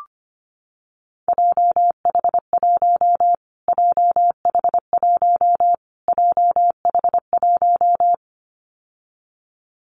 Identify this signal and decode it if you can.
Morse code, heard as a keyed tone, about 25 wpm, 700 Hz. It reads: JH1 JH1 JH1
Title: Callsign Prefixes for Top 25 DXCC - Repeated 3x - 25wpm